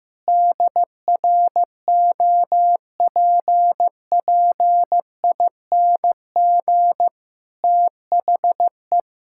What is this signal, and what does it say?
Morse code, heard as a keyed tone, about 15 wpm, 700 Hz. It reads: DROPPING THE
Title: Pangram Challenge v2 - 15wpm